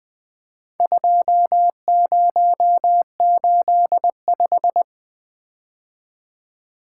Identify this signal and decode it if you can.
Morse code, heard as a keyed tone, about 20 wpm, 700 Hz. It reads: 2085
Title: Four Letter-Number Combinations - 20wpm